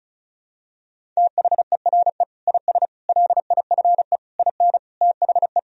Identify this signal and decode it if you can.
Morse code, heard as a keyed tone, about 36 wpm, 700 Hz. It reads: THERE IS LIFE IN THE
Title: Sentences from Top 1000 Words - Review Part 1 - 35wpm